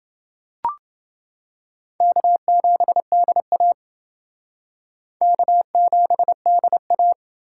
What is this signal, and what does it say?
Morse code, heard as a keyed tone, about 30 wpm, 700 Hz. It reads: K7DA K7DA